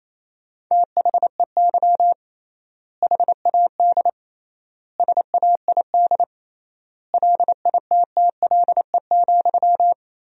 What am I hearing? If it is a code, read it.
Morse code, heard as a keyed tone, about 28 wpm, 700 Hz. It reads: THEY HAD SAID LITTLE,
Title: 3x Word-Spacing - Sentences from Top 400 Words - 28wpm